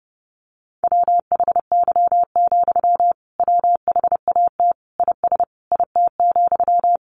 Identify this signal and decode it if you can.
Morse code, heard as a keyed tone, about 30 wpm, 700 Hz. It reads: WHY, WHAT IS IT,
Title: Sentences from Top 200 Words 30wpm